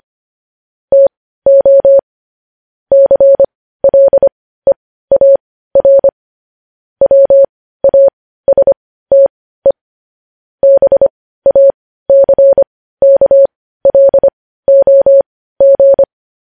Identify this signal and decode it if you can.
Morse code, heard as a keyed tone, about 25 wpm, spaced as for 15 wpm, 550 Hz. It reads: TO CLEAR WASTE BACKLOG
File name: podcast / News headlines-1x-25x15